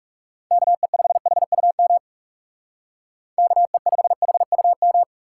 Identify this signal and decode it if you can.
Morse code, heard as a keyed tone, about 38 wpm, 700 Hz. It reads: KE5HUM KE5HUM
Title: Speed Racing - USA 2x3 Call Signs 30wpm